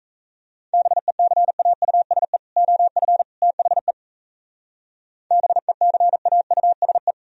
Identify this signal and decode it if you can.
Morse code, heard as a keyed tone, about 36 wpm, 700 Hz. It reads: BECAUSE OF THE BECAUSE